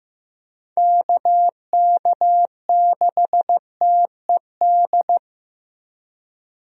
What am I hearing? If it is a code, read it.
Morse code, heard as a keyed tone, about 15 wpm, 700 Hz. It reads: KK6TED
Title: USA Call Signs - 6 Characters - 15wpm